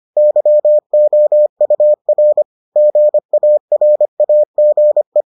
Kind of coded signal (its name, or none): Morse code